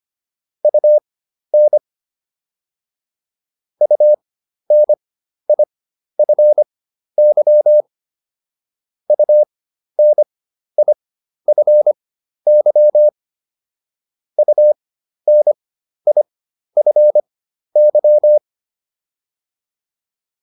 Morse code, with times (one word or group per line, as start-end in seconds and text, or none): 0.6-1.8 UN
3.8-7.8 UNIFY
9.1-13.1 UNIFY
14.4-18.4 UNIFY